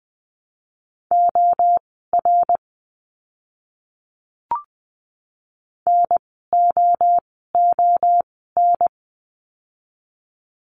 Morse code, heard as a keyed tone, about 20 wpm, spaced as for 15 wpm, 700 Hz. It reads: OR NOON
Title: 061 Lesson 11f - Letter h - Words 20wpm